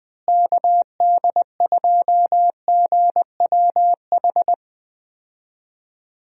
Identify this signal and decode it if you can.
Morse code, heard as a keyed tone, about 20 wpm, 700 Hz. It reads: KD2GWH
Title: USA Call Signs - 6 Characters - Repeated 3x - 20wpm